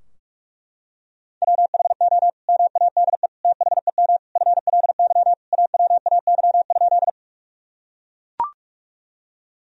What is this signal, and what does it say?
Morse code, heard as a keyed tone, about 41 wpm, 700 Hz. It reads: WHO MADE THEM FLY AWAY?
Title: Sentences from Top 1000 Words - Review Part 2 - 45wpm